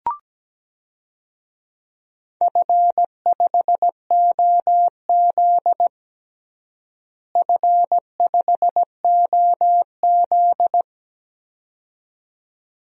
Morse code, heard as a keyed tone, about 17 wpm, 700 Hz. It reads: F5OZ F5OZ
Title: DXCC-100 Call Signs - 4 Characters - Repeated 2x - 17wpm